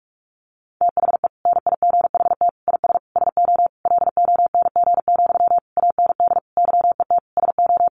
Morse code, heard as a keyed tone, about 44 wpm, 700 Hz. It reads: THE NIGHT IS SO LONG, AND YET SO
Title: Sentences from Top 500 Words 45wpm